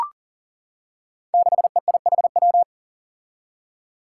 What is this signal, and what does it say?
Morse code, heard as a keyed tone, about 40 wpm, 700 Hz. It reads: 6EIHW
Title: Five Challenging Letter-Number Combinations 40wpm